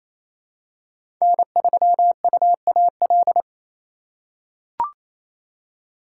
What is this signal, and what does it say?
Morse code, heard as a keyed tone, about 28 wpm, 700 Hz. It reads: N3UAL
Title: USA Call Signs - 5 Characters - 4x Character Spacing - 28wpm